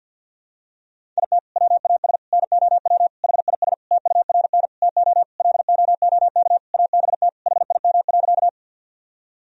Morse code, heard as a keyed tone, about 50 wpm, 700 Hz. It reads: IT WAS NOW HIS TURN TO LOOK ABT HIM.
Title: Sentences from Top 200 Words and Abbreviations 50wpm